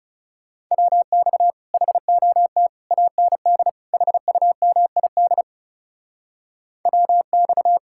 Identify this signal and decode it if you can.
Morse code, heard as a keyed tone, about 40 wpm, 700 Hz. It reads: WX HOT AND HUMID WX